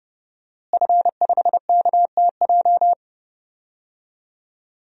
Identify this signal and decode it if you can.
Morse code, heard as a keyed tone, about 30 wpm, 700 Hz. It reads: F5KTJ
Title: DXCC-100 Call Signs - 5 Characters - Repeated 3x - 30wpm